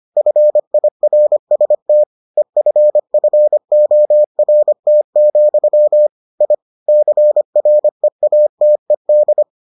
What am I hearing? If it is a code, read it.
Morse code, heard as a keyed tone, about 25 wpm, 600 Hz. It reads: FIRST EFFORT, I CREATED